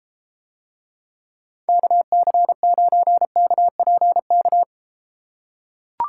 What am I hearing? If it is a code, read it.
Morse code, heard as a keyed tone, about 32 wpm, 700 Hz. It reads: KC9KPK